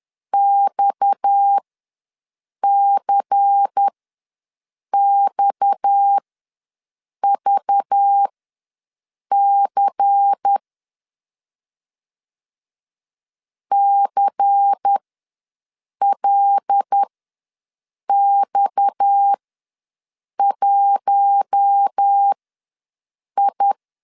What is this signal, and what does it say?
Morse code, heard as a keyed tone, about 10 wpm, spaced as for 6 wpm, 800 Hz. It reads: XCXVC CLX1I
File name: LEZIONI / 6 PKX0 / 6 PKX ws 10wpm 3706